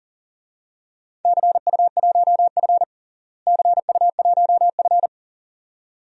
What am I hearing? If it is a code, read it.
Morse code, heard as a keyed tone, about 40 wpm, 700 Hz. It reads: CU1F CU1F
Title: DXCC-100 Call Signs - 4 Characters - Repeated 2x - 40wpm